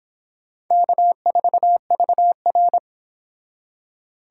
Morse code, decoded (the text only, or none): K4VR